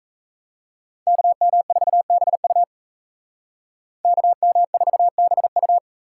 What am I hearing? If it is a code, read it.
Morse code, heard as a keyed tone, about 34 wpm, 700 Hz. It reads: KM4BU KM4BU
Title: Speed Racing - USA 2x2 Call Signs 28wpm